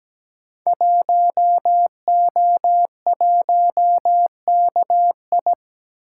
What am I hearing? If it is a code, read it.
Morse code, heard as a keyed tone, about 17 wpm, 700 Hz. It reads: 1O1KI
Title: Code Groups - 5 Letter:Numbers - 17wpm